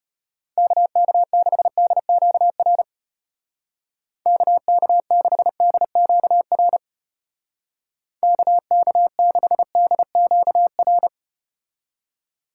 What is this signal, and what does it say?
Morse code, heard as a keyed tone, about 35 wpm, 700 Hz. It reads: KK6DQR KK6DQR KK6DQR